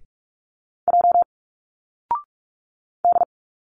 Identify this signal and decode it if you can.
Morse code, heard as a keyed tone, about 46 wpm, 700 Hz. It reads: J D
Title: Single Letters 45wpm